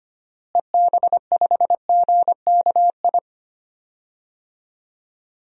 Morse code, heard as a keyed tone, about 25 wpm, 700 Hz. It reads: EB5GKI
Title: DXCC-100 Call Signs - 6 Characters - Repeated 2x - 25wpm